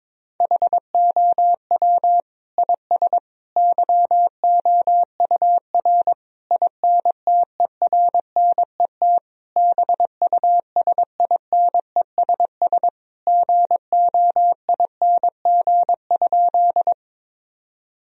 Morse code, heard as a keyed tone, about 22 wpm, 700 Hz. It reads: HOW IS YOUR INTERNET BUSINESS GOING?
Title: Sentences from Top 1000 Words - Review Part 4 - 22wpm